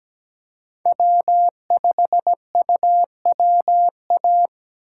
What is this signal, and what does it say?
Morse code, heard as a keyed tone, about 17 wpm, 700 Hz. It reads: W5UWA
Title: USA Call Signs - 5 Characters - 3x Character Spacing - 17wpm